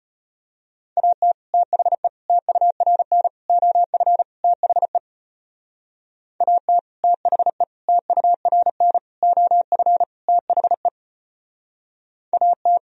Morse code, heard as a keyed tone, about 33 wpm, 700 Hz. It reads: AT THE TURN OF THE AT THE TURN OF THE AT